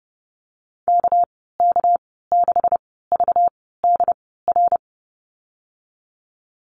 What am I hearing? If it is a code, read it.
Morse code, heard as a keyed tone, about 30 wpm, spaced as for 17 wpm, 700 Hz. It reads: KK6VDR
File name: Lesson 37g - Letter x - Call Signs 30wpm